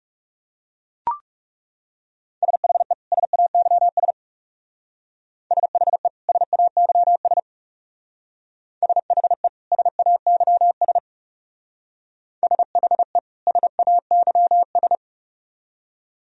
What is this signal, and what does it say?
Morse code, heard as a keyed tone, about 38 wpm, 700 Hz. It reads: SHE SAYS SHE SAYS SHE SAYS SHE SAYS